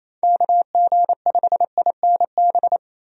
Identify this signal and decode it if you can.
Morse code, heard as a keyed tone, about 28 wpm, 700 Hz. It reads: KG5INB
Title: USA Call Signs - 6 Characters - Repeated 2x - 28wpm